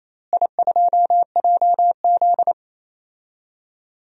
Morse code, heard as a keyed tone, about 28 wpm, 700 Hz. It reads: I2JZ
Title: DXCC-100 Call Signs - 4 Characters - Repeated 3x - 28wpm